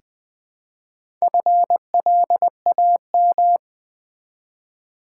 Morse code, heard as a keyed tone, about 20 wpm, 700 Hz. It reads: FLAM
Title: Build A Word - 1 Letter At A Time - 5-Letter Words - 20wpm